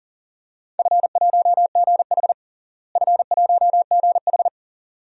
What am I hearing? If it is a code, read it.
Morse code, heard as a keyed tone, about 40 wpm, 700 Hz. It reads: F1GH F1GH
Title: DXCC-100 Call Signs - 4 Characters - Repeated 2x - 40wpm